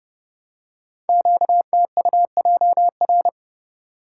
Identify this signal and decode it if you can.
Morse code, heard as a keyed tone, about 30 wpm, 700 Hz. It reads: QTUJR
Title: Code Groups - 5 Letters - 30wpm